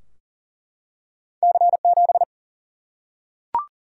Morse code, heard as a keyed tone, about 40 wpm, 700 Hz. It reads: C7
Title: Single-Letter Single-Number 40wpm